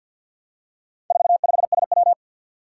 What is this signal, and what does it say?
Morse code, heard as a keyed tone, about 51 wpm, 700 Hz. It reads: 45SW
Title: Four Challenging Letter-Number Combinations 50wpm